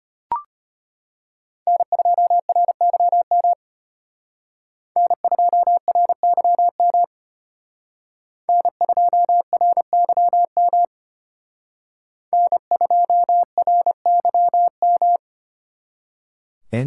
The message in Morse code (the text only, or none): N2RYM N2RYM N2RYM N2RYM